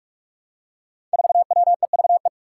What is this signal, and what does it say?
Morse code, heard as a keyed tone, about 44 wpm, 700 Hz. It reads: 4WEVE
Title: Five Challenging Letter-Number Combinations 45wpm